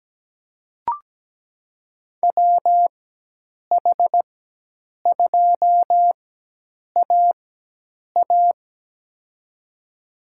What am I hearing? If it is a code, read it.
Morse code, heard as a keyed tone, about 17 wpm, spaced as for 8 wpm, 700 Hz. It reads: WH2AA
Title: USA Call Signs - 5 Characters - 4x Character Spacing - 17wpm